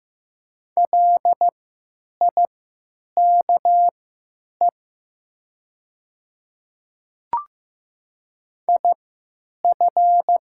Morse code, heard as a keyed tone, about 15 wpm, spaced as for 9 wpm, 700 Hz. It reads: LIKE IF
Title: 3x Character Spacing - Top 100 Words - 15wpm